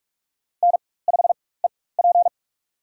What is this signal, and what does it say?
Morse code, heard as a keyed tone, about 41 wpm, spaced as for 20 wpm, 700 Hz. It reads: N5EP
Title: USA Call Signs - 4 Characters - 4x Character Spacing - 45wpm